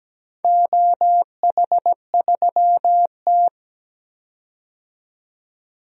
Morse code, heard as a keyed tone, about 17 wpm, 700 Hz. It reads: OH3T
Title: DXCC-100 Call Signs - 4 Characters - 17wpm